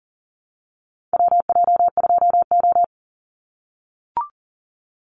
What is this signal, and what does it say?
Morse code, heard as a keyed tone, about 40 wpm, 700 Hz. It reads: WJ2O
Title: Call Signs - 40wpm